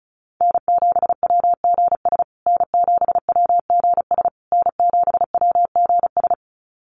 Morse code, heard as a keyed tone, about 34 wpm, 700 Hz. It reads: N7WGS N7WGS N7WGS
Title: USA Call Signs Repeated 3x 35wpm